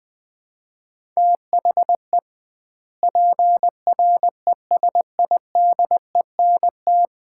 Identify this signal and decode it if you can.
Morse code, heard as a keyed tone, about 21 wpm, 700 Hz. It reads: THE PRESIDENT
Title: 2x Word-Spacing - Sets of 2 Words - 20wpm